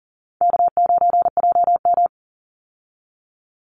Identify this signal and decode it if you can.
Morse code, heard as a keyed tone, about 40 wpm, 700 Hz. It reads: K9JM
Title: Easy Call Signs Repeated 3x - 40wpm